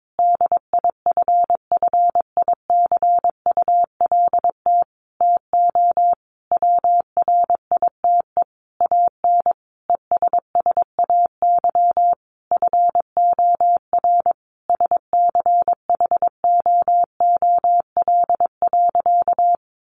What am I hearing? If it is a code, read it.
Morse code, heard as a keyed tone, about 22 wpm, 700 Hz. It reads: DIFFICULT TO WRITE AN ESSAY FOR SCHOOL.
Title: Sentences from Top 1500 Words 22wpm